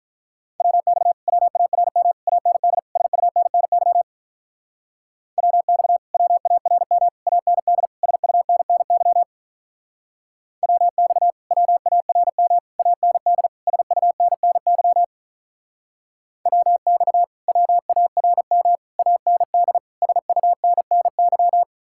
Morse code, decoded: WX WARM AND SUNNY WX WARM AND SUNNY WX WARM AND SUNNY WX WARM AND SUNNY

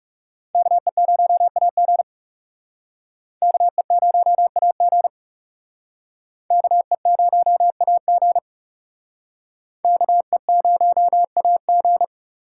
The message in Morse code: KE0AG KE0AG KE0AG KE0AG

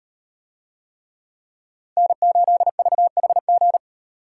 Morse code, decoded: N8VHG

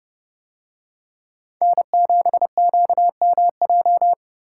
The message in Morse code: N7QMJ

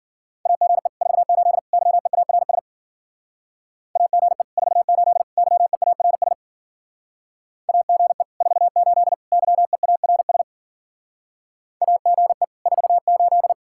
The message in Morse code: AGE 48 YEARS AGE 48 YEARS AGE 48 YEARS AGE 48

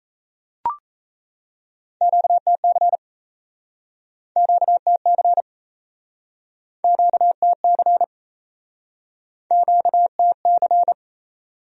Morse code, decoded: QTC QTC QTC QTC

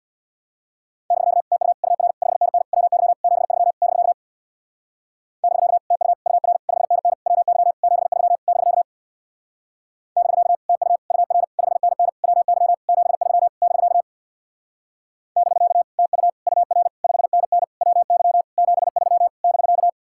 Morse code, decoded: <BK> TU UR 5NN WY 73 <BK> <BK> TU UR 5NN WY 73 <BK> <BK> TU UR 5NN WY 73 <BK> <BK> TU UR 5NN WY 73 <BK>

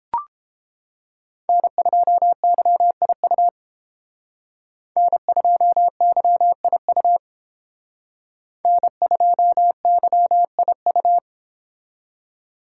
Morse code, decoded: N2YIU N2YIU N2YIU

